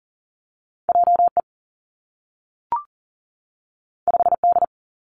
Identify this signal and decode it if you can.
Morse code, heard as a keyed tone, about 39 wpm, 700 Hz. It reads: J E 5 D
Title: Two Challenging Letter-Number Combinations 40wpm